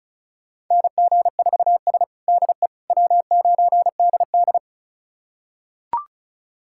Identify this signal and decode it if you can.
Morse code, heard as a keyed tone, about 36 wpm, 700 Hz. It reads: NG4S DE W9DD
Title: USA Callsign DE Callsign - 4 Characters - 35wpm